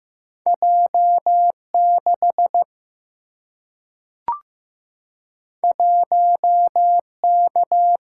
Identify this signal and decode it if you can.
Morse code, heard as a keyed tone, about 15 wpm, 700 Hz. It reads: J6 1K
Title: Single-Letter Single-Number 15wpm